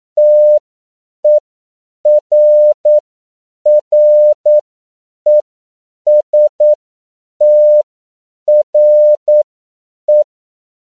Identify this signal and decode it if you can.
Morse code, heard as a keyed tone, about 9 wpm, spaced as for 7 wpm, 600 Hz. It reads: TERRESTRE
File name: CW / 10PPM